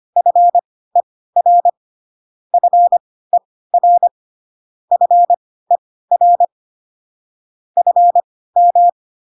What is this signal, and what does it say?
Morse code, heard as a keyed tone, about 25 wpm, spaced as for 16 wpm, 700 Hz. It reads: FER FER FER FM